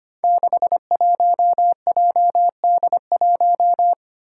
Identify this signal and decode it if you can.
Morse code, heard as a keyed tone, about 25 wpm, 700 Hz. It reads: 61JD1
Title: Five Challenging Letter-Number Combinations 25wpm